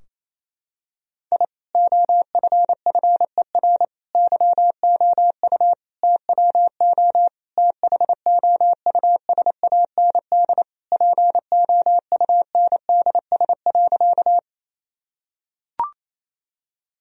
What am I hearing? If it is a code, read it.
Morse code, heard as a keyed tone, about 28 wpm, 700 Hz. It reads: I OFFER YOU TWO THOUSAND POUNDS.
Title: Sentences from Top 1000 Words - Review Part 4 - 28wpm